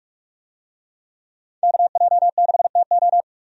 Morse code, heard as a keyed tone, about 45 wpm, 700 Hz. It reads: KJ6TO
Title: USA Call Signs - 5 Characters - 45wpm